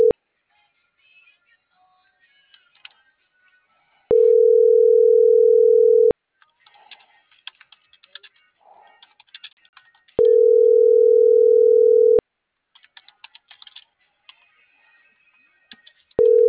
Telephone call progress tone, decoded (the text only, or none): ringback tone